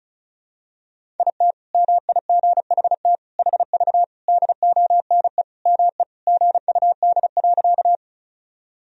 Morse code, heard as a keyed tone, about 36 wpm, 700 Hz. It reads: IT MIGHT HV DONE ME GUD.
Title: Sentences from Top 500 Words and Abbreviations 35wpm